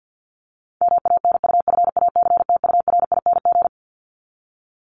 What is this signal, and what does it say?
Morse code, heard as a keyed tone, about 50 wpm, 700 Hz. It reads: MANUFACTURING
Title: ING Suffix - Top 100 Words with Base Words 50wpm